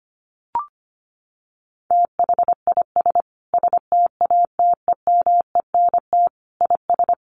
Morse code, heard as a keyed tone, about 25 wpm, 700 Hz. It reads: THIS STATEMENT IS
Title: Sentences from Top 1300 Words 25wpm